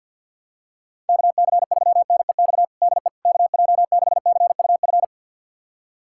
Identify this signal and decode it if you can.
Morse code, heard as a keyed tone, about 53 wpm, 700 Hz. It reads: KC3NEX DE KJ6CUF
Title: USA Callsign DE Callsign - 6 Characters - 50wpm